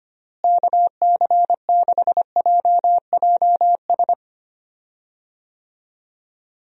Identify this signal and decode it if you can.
Morse code, heard as a keyed tone, about 25 wpm, 700 Hz. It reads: KC6JJS
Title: USA Call Signs - 6 Characters - Repeated 2x - 25wpm